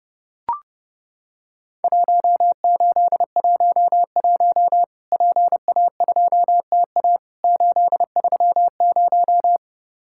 Morse code, heard as a keyed tone, about 30 wpm, 700 Hz. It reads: 1811 PA2TA 830
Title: Worked All Europe DX Contest - QTC Exchange - 30wpm